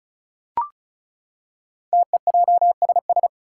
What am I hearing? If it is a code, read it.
Morse code, heard as a keyed tone, about 36 wpm, 700 Hz. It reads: TEJSS
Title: Code Groups - 5 Letters - 35wpm